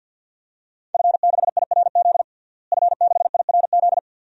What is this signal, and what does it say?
Morse code, heard as a keyed tone, about 52 wpm, 700 Hz. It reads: F6IRZ F6IRZ
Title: DXCC-100 Call Signs - 5 Characters - Repeated 2x - 50wpm